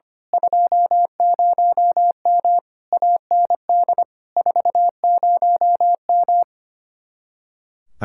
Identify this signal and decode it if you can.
Morse code, heard as a keyed tone, about 24 wpm, 700 Hz. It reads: 20M AND 40M
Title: Speed-Racing Common QSO Phrases Encore - 25wpm